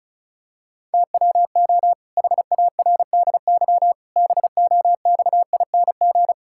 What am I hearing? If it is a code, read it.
Morse code, heard as a keyed tone, about 36 wpm, 700 Hz. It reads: TWO HARDY BOXING
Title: Pangram Challenge v2 - 35wpm